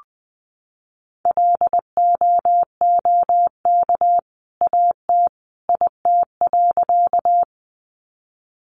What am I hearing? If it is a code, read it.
Morse code, heard as a keyed tone, about 20 wpm, 700 Hz. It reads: LOOK AT IT.
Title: Sentences from Top 100 Words 20wpm